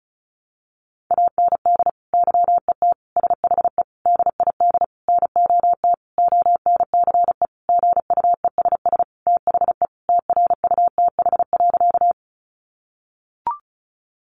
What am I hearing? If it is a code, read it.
Morse code, heard as a keyed tone, about 34 wpm, 700 Hz. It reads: AND YET SHE DID NOT ONCE GUESS THE TRUTH.
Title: Sentences from Top 1500 Words 35wpm